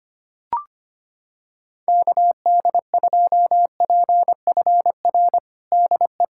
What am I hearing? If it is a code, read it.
Morse code, heard as a keyed tone, about 25 wpm, 700 Hz. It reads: KD2PFR DE
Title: USA Callsign DE Callsign - 6 Characters - 25wpm